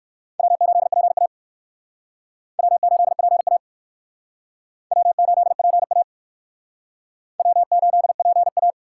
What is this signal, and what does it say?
Morse code, heard as a keyed tone, about 55 wpm, 700 Hz. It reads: W8PA W8PA W8PA W8PA
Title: Speed Racing - USA 1x2 Call Signs 45wpm